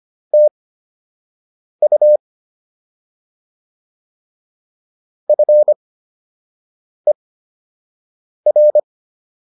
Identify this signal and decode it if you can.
Morse code, heard as a keyed tone, about 25 wpm, spaced as for 6 wpm, 600 Hz. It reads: TU FER